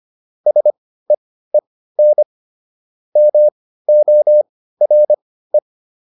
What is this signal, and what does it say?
Morse code, heard as a keyed tone, about 25 wpm, spaced as for 15 wpm, 600 Hz. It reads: SEEN MORE